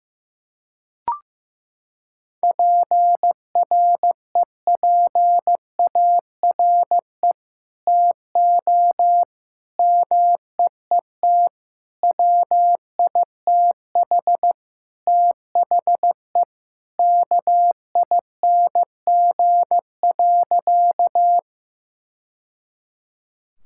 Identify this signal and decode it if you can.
Morse code, heard as a keyed tone, about 15 wpm, 700 Hz. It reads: PREPARE TO MEET WITH THE KING.
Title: Sentences from Top 1000 Words - Review Part 6 - 15wpm